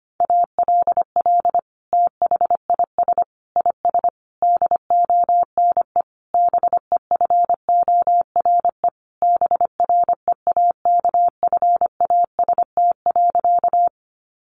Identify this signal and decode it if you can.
Morse code, heard as a keyed tone, about 25 wpm, 700 Hz. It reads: ALL THIS IS DONE BEFORE BREAKFAST.
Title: Sentences from Top 1500 Words 25wpm